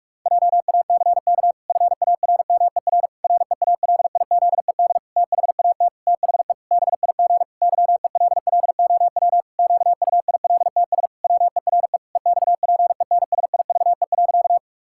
Morse code, heard as a keyed tone, about 41 wpm, 700 Hz. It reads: JACK FARMER REALIZED THAT THE BIG YELLOW QUILTS WERE EXPENSIVE.
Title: Pangram Challenge v2 - 45wpm